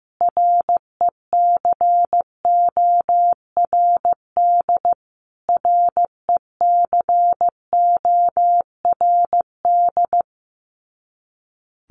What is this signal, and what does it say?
Morse code, heard as a keyed tone, about 15 wpm, 700 Hz. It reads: RECORD RECORD
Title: Top 500 Words Repeated 15wpm